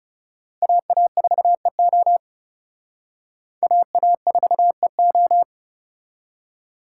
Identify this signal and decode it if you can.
Morse code, heard as a keyed tone, about 34 wpm, 700 Hz. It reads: AA4EO AA4EO
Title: Speed Racing - USA 2x2 Call Signs 30wpm